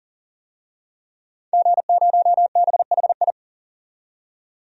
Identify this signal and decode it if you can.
Morse code, heard as a keyed tone, about 38 wpm, 700 Hz. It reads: G0BHI
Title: DXCC-100 Call Signs - 5 Characters - 40wpm